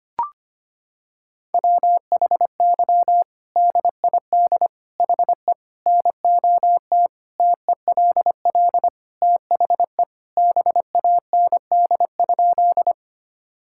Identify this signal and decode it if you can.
Morse code, heard as a keyed tone, about 25 wpm, 700 Hz. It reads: WHY DID HE NOT TELL THE BAND?
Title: Sentences from Top 1600 Words 25wpm